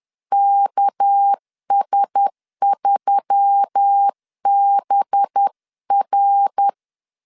Morse code, heard as a keyed tone, about 10 wpm, 800 Hz. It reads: KS3BR
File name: LEZIONI / 6 PKX0 / 6 PK 10wpm 2436